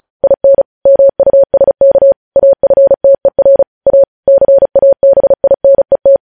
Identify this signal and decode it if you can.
Morse code, heard as a keyed tone, about 36 wpm, 550 Hz. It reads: IN MUSK AFTER A CABINET
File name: podcast / News headlines-1x-35x35